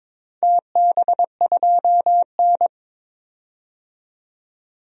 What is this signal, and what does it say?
Morse code, heard as a keyed tone, about 22 wpm, 700 Hz. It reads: TB2N
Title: DXCC-100 Call Signs - 4 Characters - 22wpm